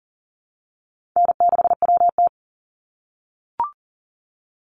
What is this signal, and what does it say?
Morse code, heard as a keyed tone, about 42 wpm, 700 Hz. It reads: N6WT
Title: NA SOTA Chaser Callsigns 40wpm